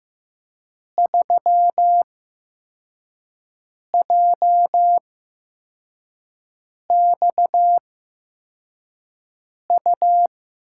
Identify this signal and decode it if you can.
Morse code, heard as a keyed tone, about 15 wpm, spaced as for 4 wpm, 700 Hz. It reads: 3JXU
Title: USA Call Signs - 8x Character Spacing - 15wpm